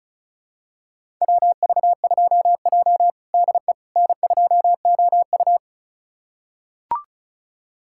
Morse code, heard as a keyed tone, about 36 wpm, 700 Hz. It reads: WV2J DE N2OU
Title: USA Callsign DE Callsign - 4 Characters - 35wpm